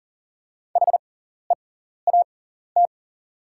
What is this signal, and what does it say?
Morse code, heard as a keyed tone, about 41 wpm, spaced as for 14 wpm, 700 Hz. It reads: HEAT
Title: 6x Character Spacing - Top 500 Words - 40wpm